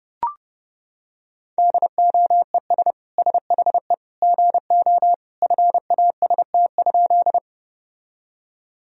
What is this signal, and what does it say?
Morse code, heard as a keyed tone, about 30 wpm, 700 Hz. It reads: DOES SHE GO FAST?
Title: Sentences from Top 1000 Words - Review Part 2 - 30wpm